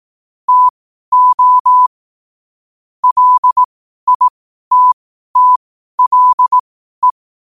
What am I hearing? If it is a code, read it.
Morse code, heard as a keyed tone, about 18 wpm, spaced as for 12 wpm, 1000 Hz. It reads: TO LITTLE
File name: qotdfiles / 12wpm